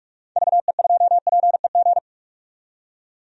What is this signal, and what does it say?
Morse code, heard as a keyed tone, about 43 wpm, 700 Hz. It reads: VE2PEG
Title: DXCC-100 Call Signs - 6 Characters - 45wpm